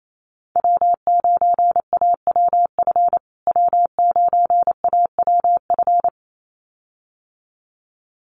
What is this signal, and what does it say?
Morse code, heard as a keyed tone, about 28 wpm, 700 Hz. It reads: W9AWF W9AWF
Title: USA Call Signs Repeated 2x 28wpm